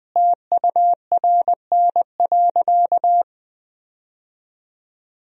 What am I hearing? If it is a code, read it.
Morse code, heard as a keyed tone, about 21 wpm, 700 Hz. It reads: TURN.
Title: Sentences from Top 1000 Words - Review Part 1 - 20wpm